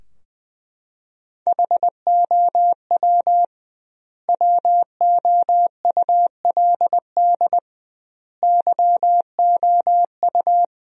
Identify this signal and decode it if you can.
Morse code, heard as a keyed tone, about 20 wpm, 700 Hz. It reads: HOW WOULD YOU
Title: 2x Word-Spacing - Sentences from Top 100 Words - 20wpm